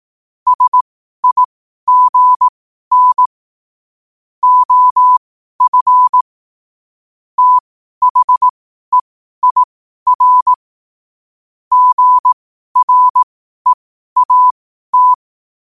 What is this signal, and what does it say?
Morse code, heard as a keyed tone, about 18 wpm, spaced as for 12 wpm, 1000 Hz. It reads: SIGN OF THEIR GREAT
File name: qotdfiles / 12wpm